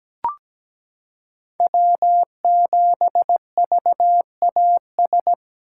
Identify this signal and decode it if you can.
Morse code, heard as a keyed tone, about 17 wpm, 700 Hz. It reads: W7VAS
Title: K1USN SST QSO Practice - 17wpm